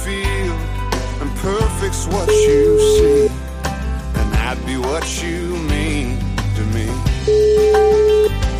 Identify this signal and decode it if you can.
Ringback tone (Europe, 425 Hz), with noise.